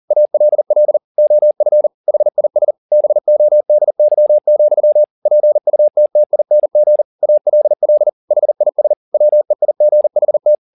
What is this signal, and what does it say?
Morse code, heard as a keyed tone, about 40 wpm, 600 Hz. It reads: ALL OF HIS BODY, PUTTING ALL HIS WEIGHT
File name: the old man and the sea 2 40wpm